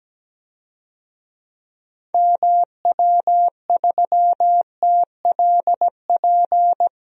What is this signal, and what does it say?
Morse code, heard as a keyed tone, about 17 wpm, 700 Hz. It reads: MW3TLP
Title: DXCC-100 Call Signs - 6 Characters - Repeated 2x - 17wpm